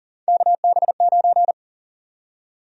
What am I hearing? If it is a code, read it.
Morse code, heard as a keyed tone, about 40 wpm, 700 Hz. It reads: KB9
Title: USA Call Sign Prefixes - Repeated 2x - 40wpm